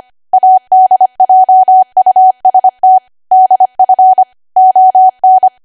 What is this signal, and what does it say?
Morse code, heard as a keyed tone, about 25 wpm, 750 Hz. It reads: ADJUST DF ON
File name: Archive / 25 WPM / 25WPM